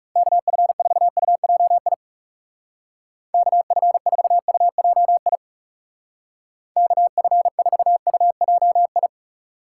Morse code, heard as a keyed tone, about 37 wpm, 700 Hz. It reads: KF4UJI KF4UJI KF4UJI
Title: Speed Racing - USA 2x3 Call Signs 30wpm